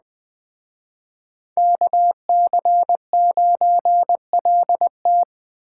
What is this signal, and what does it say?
Morse code, heard as a keyed tone, about 19 wpm, 700 Hz. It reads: KC9LT
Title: Speed Racing - USA 2x2 Call Signs 17wpm